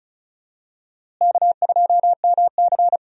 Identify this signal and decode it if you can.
Morse code, heard as a keyed tone, about 34 wpm, 700 Hz. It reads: K2MC
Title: USA Call Signs - 4 Characters - Repeated 2x - 35wpm